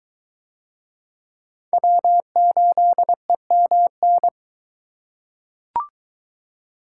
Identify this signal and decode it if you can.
Morse code, heard as a keyed tone, about 23 wpm, 700 Hz. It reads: W8EMN